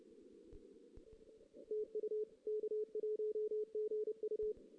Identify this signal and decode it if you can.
Morse code, heard as a keyed tone, about 30 wpm, 440 Hz. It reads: TU K1GU
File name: 2016 JAN CW NAQP / 2016 Jan naqp k5ot - 0-4